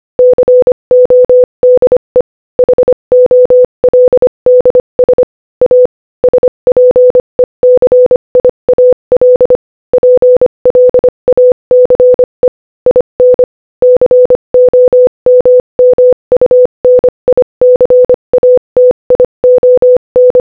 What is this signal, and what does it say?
Morse code, heard as a keyed tone, about 25 wpm, 500 Hz. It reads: CODE HOLDS A SPECIAL PLACE IN COMMUNICATION
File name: Morse-Code-Audio-Speed-25-11